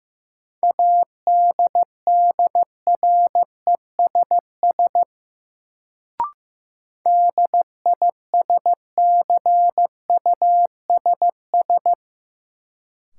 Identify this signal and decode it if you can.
Morse code, heard as a keyed tone, about 15 wpm, 700 Hz. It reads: ADDRESS DISCUSS
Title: Sentences from Top 1000 Words - New Word Drill - 15wpm